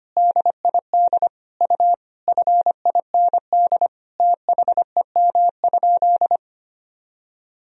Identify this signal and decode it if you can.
Morse code, heard as a keyed tone, about 25 wpm, 700 Hz. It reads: DID U FIND THEM?
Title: Sentences from Top 200 Words and Abbreviations 25wpm